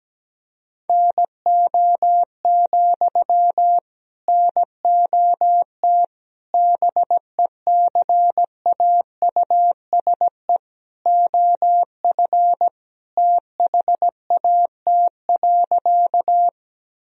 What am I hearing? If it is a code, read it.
Morse code, heard as a keyed tone, about 17 wpm, 700 Hz. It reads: NO, NOT BECAUSE OF THAT.
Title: Sentences from Top 1000 Words - Review Part 3 - 17wpm